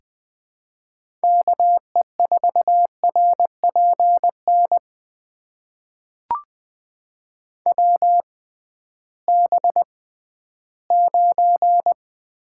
Morse code, heard as a keyed tone, about 20 wpm, spaced as for 13 wpm, 700 Hz. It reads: KE4RPN WB9
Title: USA Call Signs - 6 Characters - 6x Character Spacing - 20wpm